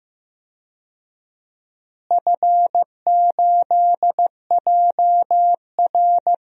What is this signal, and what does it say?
Morse code, heard as a keyed tone, about 16 wpm, 700 Hz. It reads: F8JR
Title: DXCC-100 Call Signs - 4 Characters - Repeated 3x - 15wpm